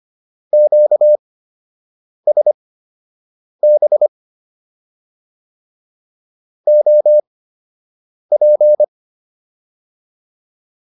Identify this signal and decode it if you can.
Morse code, heard as a keyed tone, about 25 wpm, spaced as for 7 wpm, 600 Hz. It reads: QSB OP